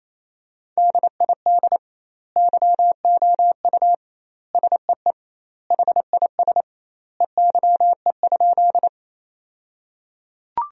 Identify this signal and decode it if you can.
Morse code, heard as a keyed tone, about 28 wpm, 700 Hz. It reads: DID YOU SEE HIS EYE?
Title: 2x Word-Spacing - Sentences from Top 300 Words - 28wpm